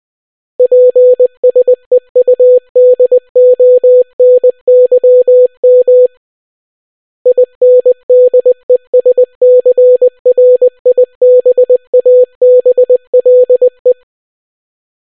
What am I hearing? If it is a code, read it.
Morse code, heard as a keyed tone, about 20 wpm, 500 Hz. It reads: PSEUDONYM INDESCRIBABLE